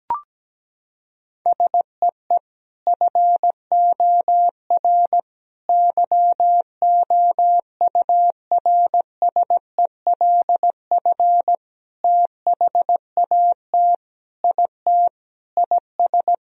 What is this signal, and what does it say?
Morse code, heard as a keyed tone, about 17 wpm, 700 Hz. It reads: SEE FOR YOURSELF THAT IT IS
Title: Sentences from Top 1000 Words - Review Part 1 - 17wpm